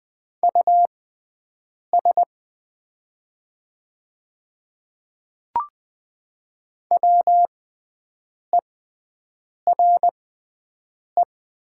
Morse code, heard as a keyed tone, about 20 wpm, spaced as for 7 wpm, 700 Hz. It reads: US WERE
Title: 6x Character Spacing - Top 300 Words - 20wpm